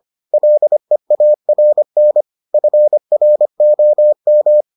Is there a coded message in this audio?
Yes: Morse code from 0.3 to 4.6 s, heard as a keyed tone, about 25 wpm, 600 Hz. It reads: LEARN FROM